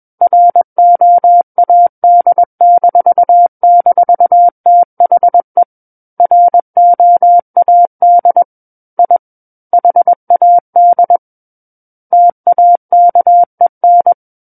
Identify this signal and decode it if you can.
Morse code, heard as a keyed tone, about 21 wpm, 700 Hz. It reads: ROAD--THE ROAD I HAD TAKEN
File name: Book One / 12-Chapter 12 - 21 WPM